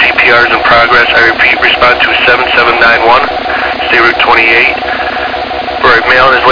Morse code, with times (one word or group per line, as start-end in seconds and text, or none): none